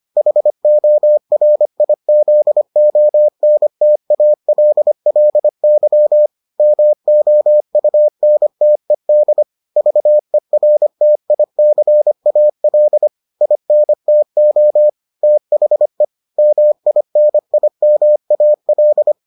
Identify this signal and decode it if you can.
Morse code, heard as a keyed tone, about 25 wpm, 600 Hz. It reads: HORIZONTALLY MOUNTED VERTICAL INTO THE MINIMAL